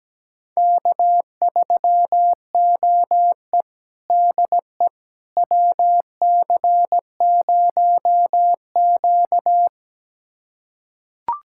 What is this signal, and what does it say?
Morse code, heard as a keyed tone, about 17 wpm, 700 Hz. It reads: K3OE DE WC0Q
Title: USA Callsign DE Callsign - 4 Characters - 17wpm